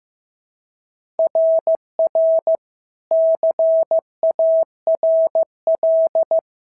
Morse code, heard as a keyed tone, about 14 wpm, 650 Hz. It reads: RR CARL
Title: SOTA QSO Practice 15wpm